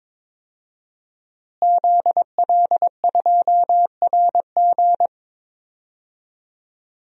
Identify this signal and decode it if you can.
Morse code, heard as a keyed tone, about 22 wpm, 700 Hz. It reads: ZL2RG
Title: DXCC-100 Call Signs - 5 Characters - Repeated 3x - 22wpm